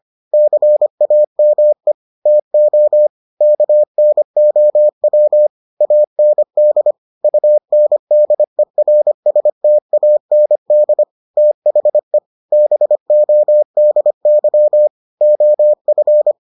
Morse code, heard as a keyed tone, about 25 wpm, 600 Hz. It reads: CAME TO KNOW AND UNDERSTAND THE BODY OF